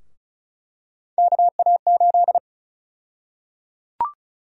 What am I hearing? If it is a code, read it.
Morse code, heard as a keyed tone, about 36 wpm, 700 Hz. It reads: KA8
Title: USA Call Sign Prefixes - Repeated 2x - 35wpm